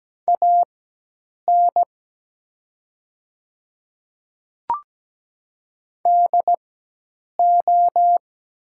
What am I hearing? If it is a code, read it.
Morse code, heard as a keyed tone, about 17 wpm, spaced as for 8 wpm, 690 Hz. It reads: AN DO
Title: 4x Character Spacing - Top 200 Words - 17wpm